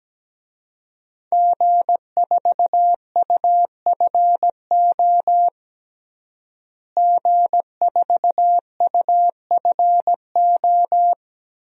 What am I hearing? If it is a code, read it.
Morse code, heard as a keyed tone, about 17 wpm, 700 Hz. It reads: G4UFO G4UFO
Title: DXCC-100 Call Signs - 5 Characters - Repeated 2x - 17wpm